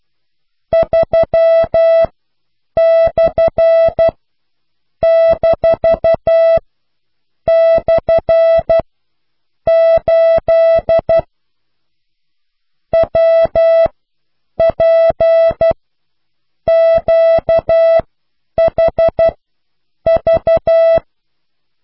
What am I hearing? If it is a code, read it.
Morse code, heard as a keyed tone, about 12 wpm, spaced as for 7 wpm, 660 Hz. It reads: 3/-/8 WPQHV